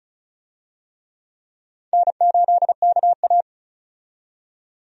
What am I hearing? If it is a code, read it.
Morse code, heard as a keyed tone, about 36 wpm, 700 Hz. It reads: N8KA
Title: USA Call Signs - 4 Characters - 35wpm